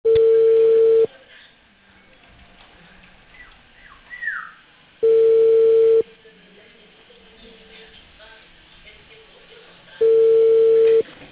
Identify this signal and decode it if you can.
Ringback tone (China, 450 Hz).